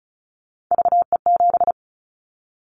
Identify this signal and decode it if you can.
Morse code, heard as a keyed tone, about 36 wpm, 700 Hz. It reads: VE7
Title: Callsign Prefixes for Top 25 DXCC - 35wpm